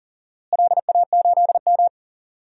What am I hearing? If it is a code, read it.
Morse code, heard as a keyed tone, about 40 wpm, 700 Hz. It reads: LA8M
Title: DXCC-100 Call Signs - 4 Characters - Repeated 2x - 40wpm